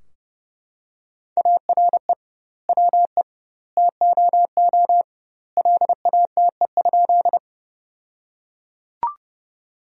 Morse code, heard as a keyed tone, about 30 wpm, 700 Hz. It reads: ARE WE TOO LATE?
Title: 2x Word-Spacing - Sentences from Top 300 Words - 30wpm